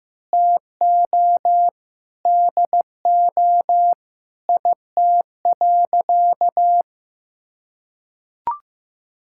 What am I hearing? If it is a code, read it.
Morse code, heard as a keyed tone, about 15 wpm, 700 Hz. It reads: TO DO IT.
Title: Sentences from Top 1000 Words - Review Part 6 - 15wpm